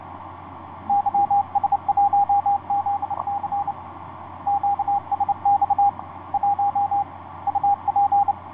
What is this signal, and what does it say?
Morse code, heard as a keyed tone, about 29 wpm, 810 Hz, with radio noise. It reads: YS1ZC QSX 1 UP
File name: ys1zc ik2cik etc 30m cw 07feb2010 2249